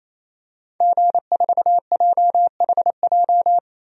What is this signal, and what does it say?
Morse code, heard as a keyed tone, about 27 wpm, 700 Hz. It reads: G4JHJ
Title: DXCC-100 Call Signs - 5 Characters - Repeated 3x - 28wpm